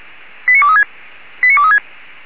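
Five-tone selective call (ZVEI), heard as quick runs of five tones, tones of about 70 ms. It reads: 89236 89236
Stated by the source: ZVEI five-tone selective call received on SDR.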